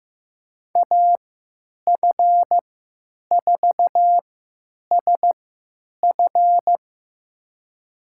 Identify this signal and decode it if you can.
Morse code, heard as a keyed tone, about 16 wpm, spaced as for 9 wpm, 700 Hz. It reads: AF4SF
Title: USA Call Signs - 5 Characters - 3x Character Spacing - 15wpm